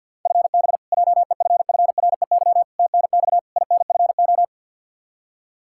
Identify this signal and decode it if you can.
Morse code, heard as a keyed tone, about 45 wpm, 700 Hz. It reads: FB JEFFREY TNX INFO
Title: K1USN SST QSO Practice 50wpm